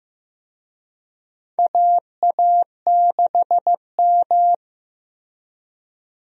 Morse code, heard as a keyed tone, about 16 wpm, 700 Hz. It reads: AA6M
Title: USA Call Signs - 4 Characters - Repeated 3x - 15wpm